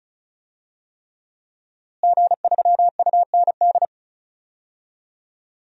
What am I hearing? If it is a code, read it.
Morse code, heard as a keyed tone, about 36 wpm, 700 Hz. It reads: G3UND
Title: DXCC-100 Call Signs - 5 Characters - Repeated 3x - 35wpm